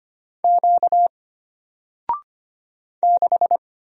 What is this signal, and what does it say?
Morse code, heard as a keyed tone, about 25 wpm, 700 Hz. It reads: Q 6
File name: Single-Letter-Number-25wpm